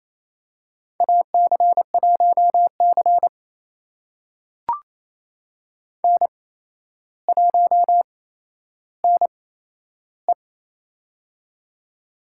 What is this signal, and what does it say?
Morse code, heard as a keyed tone, about 28 wpm, spaced as for 12 wpm, 700 Hz. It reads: AC1C N1NE
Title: USA Call Signs - 4 Characters - 8x Character Spacing - 28wpm